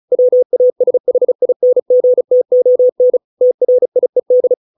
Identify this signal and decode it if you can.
Morse code, heard as a keyed tone, about 34 wpm, 500 Hz. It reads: WASHINGTON TRIED